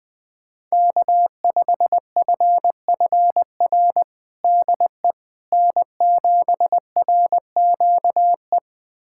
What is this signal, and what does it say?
Morse code, heard as a keyed tone, about 21 wpm, 700 Hz. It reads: K5FFR DE N7RQE
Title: USA Callsign DE Callsign - 5 Characters - 20wpm